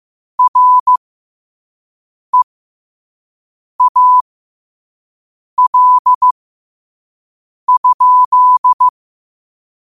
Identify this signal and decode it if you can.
Morse code, heard as a keyed tone, about 15 wpm, spaced as for 5 wpm, 1000 Hz. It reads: REAL?